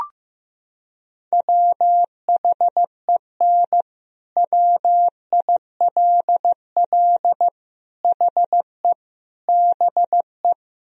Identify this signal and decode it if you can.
Morse code, heard as a keyed tone, about 16 wpm, 700 Hz. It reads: WHEN WILL HE BE
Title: Sentences from Top 1000 Words - Review Part 4 - 15wpm